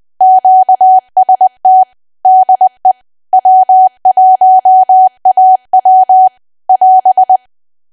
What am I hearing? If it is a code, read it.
Morse code, heard as a keyed tone, about 20 wpm, 750 Hz. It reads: QST DE W1AW <AS>
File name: cwa / W1AW-Level3 / 20WPM